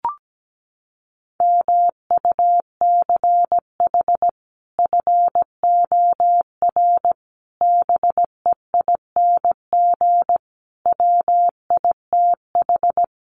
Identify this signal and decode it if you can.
Morse code, heard as a keyed tone, about 17 wpm, 700 Hz. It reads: MUCH FOR BEING WITH
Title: Sets of 5 Words 17wpm